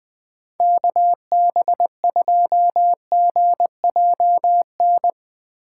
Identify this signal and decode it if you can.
Morse code, heard as a keyed tone, about 21 wpm, 700 Hz. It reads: KB2GJN
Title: USA Call Signs - 2x Character Spacing - 20wpm